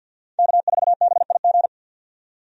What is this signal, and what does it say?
Morse code, heard as a keyed tone, about 47 wpm, 700 Hz. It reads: K4BIG
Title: USA Call Signs - 5 Characters - 50wpm